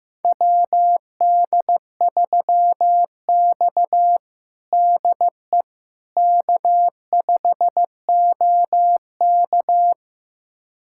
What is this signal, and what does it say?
Morse code, heard as a keyed tone, about 15 wpm, 700 Hz. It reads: WD3X DE K5OK
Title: USA Callsign DE Callsign - 4 Characters - 15wpm